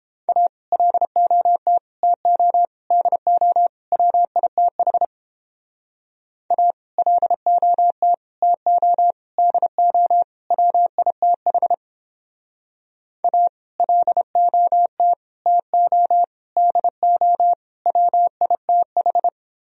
Morse code, heard as a keyed tone, about 29 wpm, 700 Hz. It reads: A LOT TO DO WITH A LOT TO DO WITH A LOT TO DO WITH